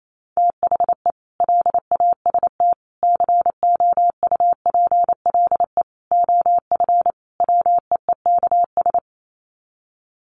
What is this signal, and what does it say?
Morse code, heard as a keyed tone, about 28 wpm, 700 Hz. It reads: THE LAST COUPLE OF WEEKS
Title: Sets of 5 Words Encore2 28wpm